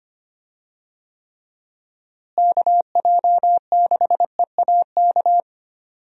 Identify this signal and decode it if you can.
Morse code, heard as a keyed tone, about 25 wpm, 700 Hz. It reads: KJ6EAK
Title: USA Call Signs - 6 Characters - Repeated 2x - 25wpm